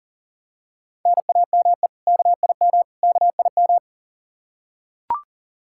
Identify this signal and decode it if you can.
Morse code, heard as a keyed tone, about 40 wpm, 700 Hz. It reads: NAME KIM KIM
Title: K1USN SST QSO Practice 40wpm